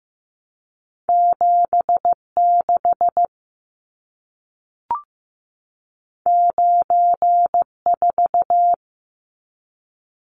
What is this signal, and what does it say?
Morse code, heard as a keyed tone, about 15 wpm, 700 Hz. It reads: 7 6 9 4
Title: Numbers - 2-Digit 15wpm